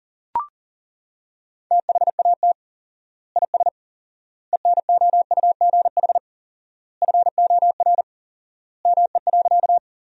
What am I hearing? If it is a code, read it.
Morse code, heard as a keyed tone, about 40 wpm, 700 Hz. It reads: THAT IS ENOUGH FOR ME.
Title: 4x Word-Spacing - Sentences from Top 400 Words - 40wpm